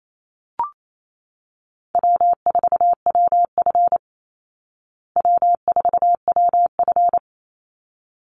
Morse code, heard as a keyed tone, about 28 wpm, 700 Hz. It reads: W4WF W4WF
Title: Easy Call Signs Repeated 2x - 28wpm